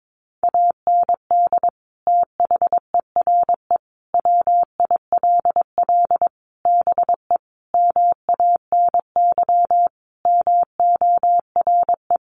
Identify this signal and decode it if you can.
Morse code, heard as a keyed tone, about 22 wpm, 700 Hz. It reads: AND THERE WILL BE MANY MORE
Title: Sentences from Top 1500 Words 22wpm